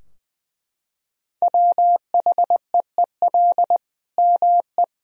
Morse code, heard as a keyed tone, about 20 wpm, 700 Hz. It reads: WHEEL ME
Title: Sentences from Top 1000 Words - Review Part 3 - 20wpm